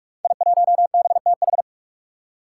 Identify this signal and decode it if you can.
Morse code, heard as a keyed tone, about 44 wpm, 700 Hz. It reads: I1BTH
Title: Five Challenging Letter-Number Combinations 45wpm